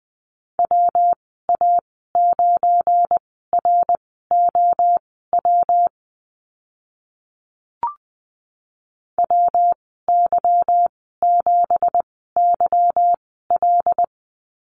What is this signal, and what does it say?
Morse code, heard as a keyed tone, about 20 wpm, spaced as for 15 wpm, 700 Hz. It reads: WA9ROW WY7YL
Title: 173 Lesson 28g - Number 9 - Call Signs 20wpm